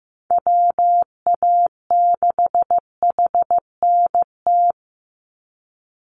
Morse code, heard as a keyed tone, about 15 wpm, 700 Hz. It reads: WA6HNT
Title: USA Call Signs 15wpm